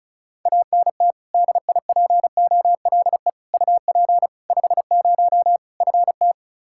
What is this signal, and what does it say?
Morse code, heard as a keyed tone, about 34 wpm, 700 Hz. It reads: ANT DIPOLE UP 50 FT
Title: Practice QSO - Exchange 2 - Info - 35wpm